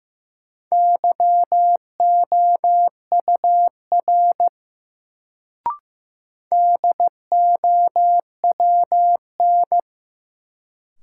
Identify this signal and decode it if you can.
Morse code, heard as a keyed tone, about 15 wpm, 700 Hz. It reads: YOUR DOWN
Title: Sets of 2 Words - New Word Drill - 15wpm